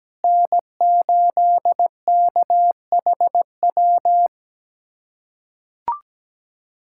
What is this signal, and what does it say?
Morse code, heard as a keyed tone, about 17 wpm, 700 Hz. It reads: N8KHW
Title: USA Callsign DE Callsign - 5 Characters - 17wpm